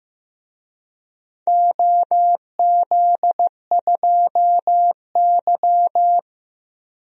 Morse code, heard as a keyed tone, about 15 wpm, 700 Hz. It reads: OZ2Y
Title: DXCC-100 Call Signs - 4 Characters - 15wpm